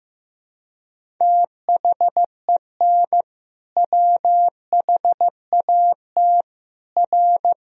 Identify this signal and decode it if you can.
Morse code, heard as a keyed tone, about 15 wpm, 700 Hz. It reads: THEN WHAT R
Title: Sentences from Top 100 Words and Abbreviations 15wpm